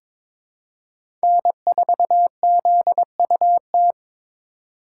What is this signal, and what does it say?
Morse code, heard as a keyed tone, about 23 wpm, 700 Hz. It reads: N4ZUT
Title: USA Call Signs - 5 Characters - 2x Character Spacing - 22wpm